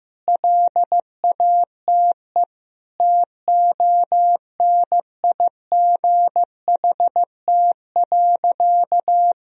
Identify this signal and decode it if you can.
Morse code, heard as a keyed tone, about 15 wpm, 700 Hz. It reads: LATE TONIGHT.
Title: Sentences from Top 1000 Words - Review Part 4 - 15wpm